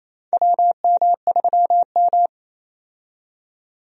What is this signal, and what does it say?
Morse code, heard as a keyed tone, about 28 wpm, 690 Hz. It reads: WM3M
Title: USA Call Signs - 4 Characters - 28wpm